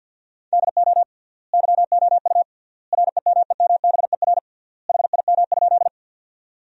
Morse code, heard as a keyed tone, about 52 wpm, 700 Hz. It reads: DO YOU REMEMBER HIM?
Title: 3x Word-Spacing - Sentences from Top 400 Words - 50wpm